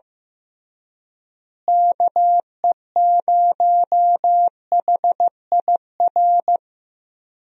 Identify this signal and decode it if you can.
Morse code, heard as a keyed tone, about 15 wpm, 700 Hz. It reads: KE0HIR
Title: USA Call Signs - 6 Characters - Repeated 3x - 15wpm